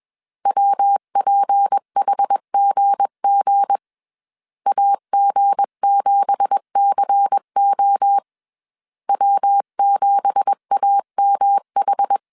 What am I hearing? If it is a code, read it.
Morse code, heard as a keyed tone, about 21 wpm, 800 Hz. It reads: WPHGG AG7CO W7AMH
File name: CW / DET / 1 9 1500C S C 20wpm 2310